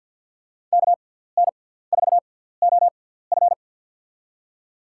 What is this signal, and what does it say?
Morse code, heard as a keyed tone, about 47 wpm, spaced as for 17 wpm, 700 Hz. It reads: KN4OF
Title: USA Call Signs - 6x Character Spacing - 50wpm